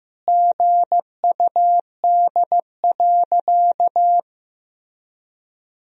Morse code, heard as a keyed tone, about 14 wpm, 700 Hz. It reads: GUD.
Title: Sentences from Top 200 Words and Abbreviations 15wpm